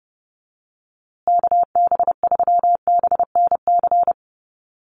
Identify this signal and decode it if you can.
Morse code, heard as a keyed tone, about 30 wpm, 700 Hz. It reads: KB3BNC
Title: USA Call Signs Repeated 3x 30wpm